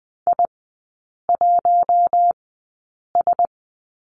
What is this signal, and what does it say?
Morse code, heard as a keyed tone, about 20 wpm, spaced as for 8 wpm, 700 Hz. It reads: I1S
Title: 053 Lesson 10e - Letter r - Three Characters 20wpm